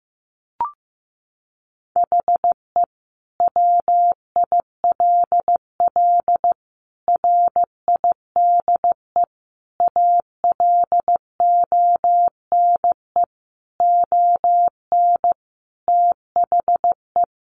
Morse code, heard as a keyed tone, about 15 wpm, 700 Hz. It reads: HE WILL RIDE ALONE ON THE
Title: Sentences from Top 1400 Words 15wpm